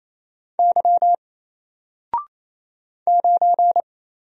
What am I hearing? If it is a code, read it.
Morse code, heard as a keyed tone, about 29 wpm, 700 Hz. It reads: Y 9
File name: Single-Letter-Number-28wpm